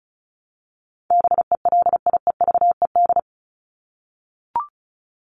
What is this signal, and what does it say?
Morse code, heard as a keyed tone, about 34 wpm, 700 Hz. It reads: BELIEVED
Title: ED Suffix - Top 100 Words with Base Words 35wpm